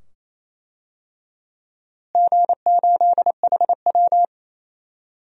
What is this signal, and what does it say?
Morse code, heard as a keyed tone, about 28 wpm, 700 Hz. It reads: G8HW
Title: DXCC-100 Call Signs - 4 Characters - Repeated 2x - 28wpm